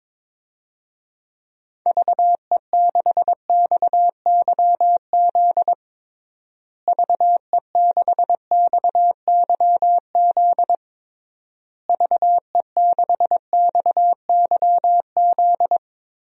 Morse code, heard as a keyed tone, about 22 wpm, 700 Hz. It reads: VE6XYZ VE6XYZ VE6XYZ
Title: DXCC-100 Call Signs - 6 Characters - Repeated 3x - 22wpm